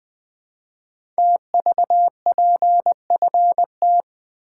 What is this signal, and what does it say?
Morse code, heard as a keyed tone, about 20 wpm, 700 Hz. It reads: TVPFT
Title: Code Groups - 5 Letters - 20wpm